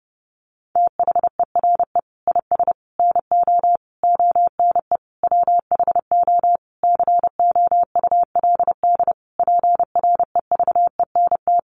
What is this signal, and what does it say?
Morse code, heard as a keyed tone, about 30 wpm, 700 Hz. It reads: THERE IS NO ONE WHO COULD PREVENT
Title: Sentences from Top 1100 Words 30wpm